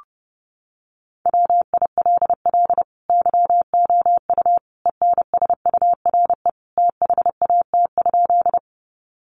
Morse code, heard as a keyed tone, about 30 wpm, 700 Hz. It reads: WILL YOU ENSURE THAT?
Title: Sentences from Top 1100 Words 30wpm